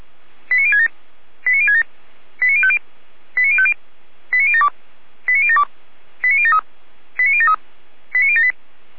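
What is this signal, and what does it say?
Five-tone selective call (ZVEI), heard as quick runs of five tones, tones of about 70 ms. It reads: 89067 89067 89050 89050 89072 89072 89073 89073 89078
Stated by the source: ZVEI five-tone call received on an SDR.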